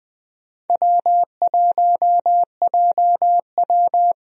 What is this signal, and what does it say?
Morse code, heard as a keyed tone, about 20 wpm, 700 Hz. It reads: W1JW
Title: USA Call Signs - 4 Characters - 20wpm